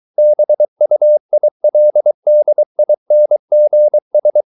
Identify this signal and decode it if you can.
Morse code, heard as a keyed tone, about 23 wpm, 600 Hz. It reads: BUILDINGS